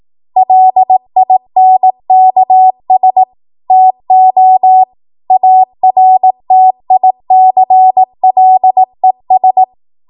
Morse code, heard as a keyed tone, about 18 wpm, 750 Hz. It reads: LINKS TO ARTICLES
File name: Morse / 18WPM